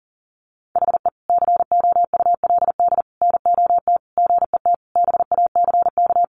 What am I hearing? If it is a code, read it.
Morse code, heard as a keyed tone, about 40 wpm, 700 Hz. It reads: HE COULD NOT GET BACK
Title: Sentences from Top 1100 Words 40wpm